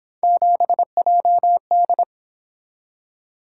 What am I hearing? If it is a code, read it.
Morse code, heard as a keyed tone, about 27 wpm, 700 Hz. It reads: 7JD
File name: Speed-Racing - Three Letter-Number Combinations - 22wpm